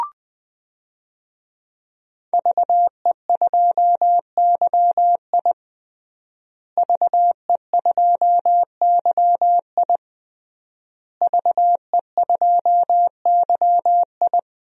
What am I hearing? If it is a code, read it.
Morse code, heard as a keyed tone, about 21 wpm, 700 Hz. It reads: VE2YI VE2YI VE2YI
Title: DXCC-100 Call Signs - 5 Characters - Repeated 3x - 20wpm